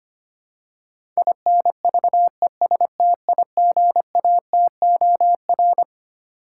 Morse code, heard as a keyed tone, about 25 wpm, 700 Hz. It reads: INVESTIGATOR
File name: Twelve Letter Words - 25wpm